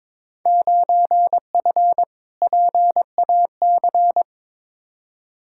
Morse code, heard as a keyed tone, about 22 wpm, 700 Hz. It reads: 9F PAC
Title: Field Day Exchanges Encore - 22wpm